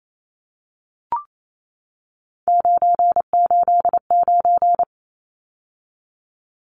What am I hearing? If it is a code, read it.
Morse code, heard as a keyed tone, about 28 wpm, 700 Hz. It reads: 989
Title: Numbers - 3-Digit 28wpm